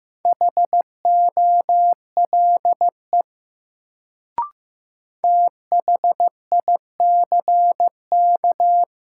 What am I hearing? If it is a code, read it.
Morse code, heard as a keyed tone, about 15 wpm, 700 Hz. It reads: HOLE THICK
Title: Sentences from Top 1600 Words - New Word Drill Alt - 15wpm